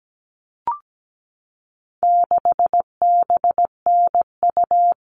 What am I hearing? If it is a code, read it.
Morse code, heard as a keyed tone, about 17 wpm, 700 Hz. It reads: 6BNU
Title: All Cut Numbers - 4-Digit 17wpm